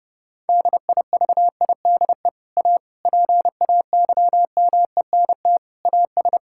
Morse code, heard as a keyed tone, about 32 wpm, 700 Hz. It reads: DIVIDE A PAYMENT AS
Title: Sentences from Top 1600 Words 30wpm